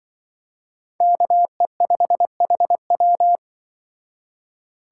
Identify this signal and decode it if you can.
Morse code, heard as a keyed tone, about 25 wpm, 700 Hz. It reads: KE5HW
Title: Speed Racing - USA 2x2 Call Signs 20wpm